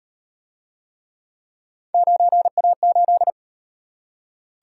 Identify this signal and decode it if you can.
Morse code, heard as a keyed tone, about 34 wpm, 700 Hz. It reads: 9A8